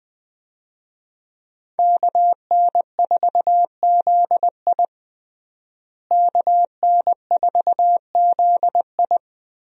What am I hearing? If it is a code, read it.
Morse code, heard as a keyed tone, about 20 wpm, 700 Hz. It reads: KN4ZI KN4ZI
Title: USA Call Signs - 5 Characters - Repeated 2x - 20wpm